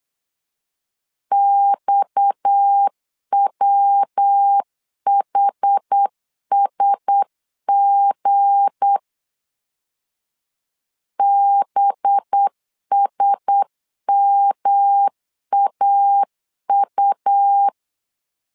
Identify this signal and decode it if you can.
Morse code, heard as a keyed tone, about 8 wpm, 800 Hz. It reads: XWHSG BSMAU
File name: LEZIONI / 7 GF9Z / 7 GF 8wpm 3108